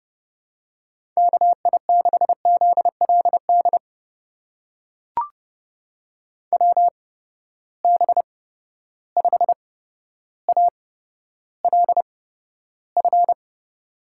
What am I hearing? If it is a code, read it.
Morse code, heard as a keyed tone, about 30 wpm, spaced as for 13 wpm, 700 Hz. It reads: KI6ZLD WB5ALF
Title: USA Call Signs - 6 Characters - 8x Character Spacing - 30wpm